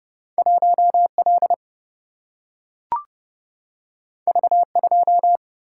Morse code, heard as a keyed tone, about 30 wpm, 700 Hz. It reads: 1 L V 2
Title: Single-Letter Single-Number 30wpm